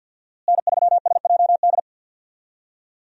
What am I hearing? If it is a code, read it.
Morse code, heard as a keyed tone, about 50 wpm, 700 Hz. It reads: N3SJD
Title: USA Call Signs - 5 Characters - Repeated 3x - 50wpm